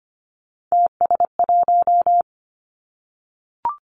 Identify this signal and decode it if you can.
Morse code, heard as a keyed tone, about 25 wpm, 700 Hz. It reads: TS1
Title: Three Challenging Letter-Number Combinations 25wpm